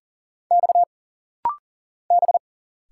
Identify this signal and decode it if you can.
Morse code, heard as a keyed tone, about 40 wpm, 700 Hz. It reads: X B
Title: Single Letters - Rapid-Fire 40wpm